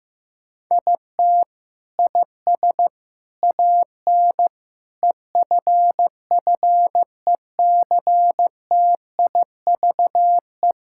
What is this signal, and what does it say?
Morse code, heard as a keyed tone, about 16 wpm, 700 Hz. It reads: IT IS AN EFFECTIVE
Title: Sentences from Top 1000 Words - Review Part 6 - 15wpm